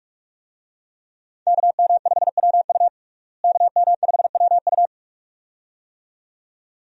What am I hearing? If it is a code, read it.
Morse code, heard as a keyed tone, about 42 wpm, 700 Hz. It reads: KM5WU KM5WU
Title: USA Call Signs - 5 Characters - Repeated 2x - 45wpm